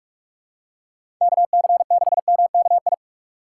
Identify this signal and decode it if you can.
Morse code, heard as a keyed tone, about 47 wpm, 700 Hz. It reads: KC6MKI
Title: USA Call Signs - 6 Characters - 4x Character Spacing - 45wpm